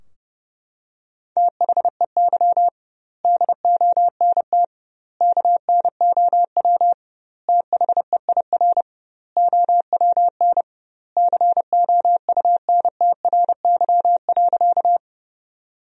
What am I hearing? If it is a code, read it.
Morse code, heard as a keyed tone, about 30 wpm, 700 Hz. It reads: THEY DONT KNOW THEIR OWN COUNTRY.
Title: 2x Word-Spacing - Sentences from Top 300 Words - 30wpm